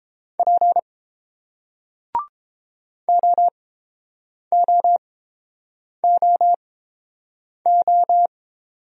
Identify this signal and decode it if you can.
Morse code, heard as a keyed tone, about 30 wpm, spaced as for 7 wpm, 700 Hz. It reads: P OOOO